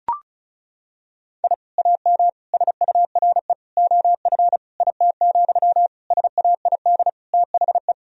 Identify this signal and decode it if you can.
Morse code, heard as a keyed tone, about 34 wpm, 700 Hz. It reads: I AM SURE OF IT, SAID THE
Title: Sentences from Top 1000 Words - Review Part 4 - 35wpm